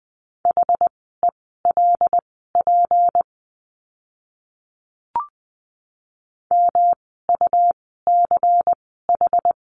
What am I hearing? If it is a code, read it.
Morse code, heard as a keyed tone, about 20 wpm, spaced as for 14 wpm, 700 Hz. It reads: HELP MUCH
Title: 237 Lesson 38f - Letter q - Words 20wpm